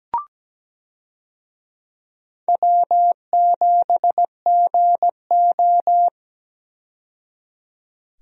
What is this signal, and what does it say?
Morse code, heard as a keyed tone, about 17 wpm, 700 Hz. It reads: W7GO
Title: USA Call Signs - 4 Characters - 17wpm